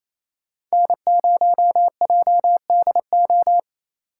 Morse code, heard as a keyed tone, about 35 wpm, 700 Hz. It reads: N0JDO